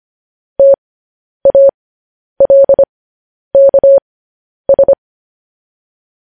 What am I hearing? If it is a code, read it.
Morse code, heard as a keyed tone, about 25 wpm, spaced as for 10 wpm, 550 Hz. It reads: TALKS
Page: news headlines in morse code at 10 wpm